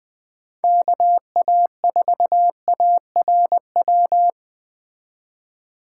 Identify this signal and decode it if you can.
Morse code, heard as a keyed tone, about 20 wpm, 700 Hz. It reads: KA4ARW